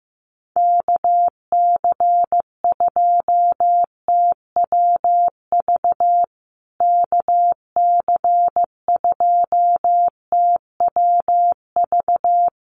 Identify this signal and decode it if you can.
Morse code, heard as a keyed tone, about 14 wpm, 700 Hz. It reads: KC2TWV KC2TWV
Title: USA Call Signs Repeated 3x 15wpm